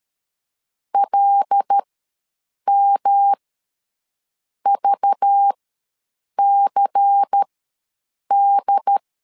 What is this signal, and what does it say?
Morse code, heard as a keyed tone, about 13 wpm, spaced as for 7 wpm, 800 Hz. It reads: LMVCD
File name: LEZIONI / 5 LBVC / 5 LBVC ws 12wpm 3159